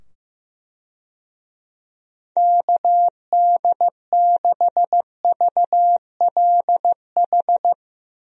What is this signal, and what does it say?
Morse code, heard as a keyed tone, about 15 wpm, 700 Hz. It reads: KD6VLH
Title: USA Call Signs - 6 Characters - 15wpm